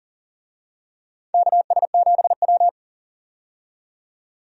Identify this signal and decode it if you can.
Morse code, heard as a keyed tone, about 40 wpm, 700 Hz. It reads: KS7W
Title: USA Call Signs - 4 Characters - 40wpm